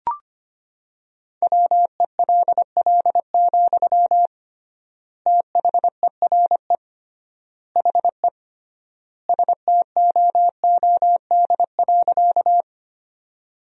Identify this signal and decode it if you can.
Morse code, heard as a keyed tone, about 25 wpm, 700 Hz. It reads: WELL, THERE HE STOOD.
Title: 3x Word-Spacing - Sentences from Top 500 Words - 25wpm